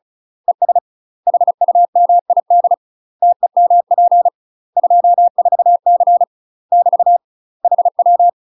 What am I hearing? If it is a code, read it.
Morse code, heard as a keyed tone, about 35 wpm, 700 Hz. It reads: ES HUMID TEMP 24C <BT> HW
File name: UK MASTER QSOs TRAINING / 35WPM / UK MASTER QSOs 0014